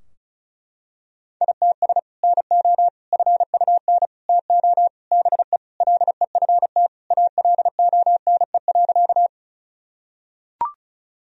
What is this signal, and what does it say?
Morse code, heard as a keyed tone, about 36 wpm, 700 Hz. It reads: ITS NO FUN TO BE LEFT ALONE.
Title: Sentences from Top 1000 Words - Review Part 4 - 35wpm